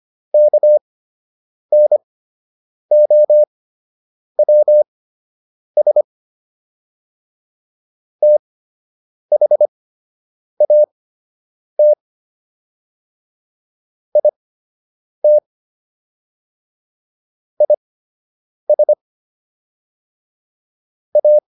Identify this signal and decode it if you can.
Morse code, heard as a keyed tone, about 25 wpm, spaced as for 8 wpm, 600 Hz. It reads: KNOWS THAT IT IS A